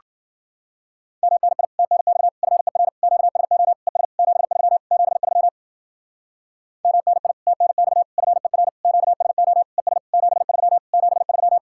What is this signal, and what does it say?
Morse code, heard as a keyed tone, about 51 wpm, 700 Hz. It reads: MNI TNX FER QSO ES 73 73 MNI TNX FER QSO ES 73 73